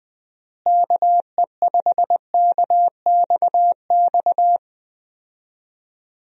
Morse code, heard as a keyed tone, about 20 wpm, 700 Hz. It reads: KE5KXX
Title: USA Call Signs - 6 Characters - Repeated 2x - 20wpm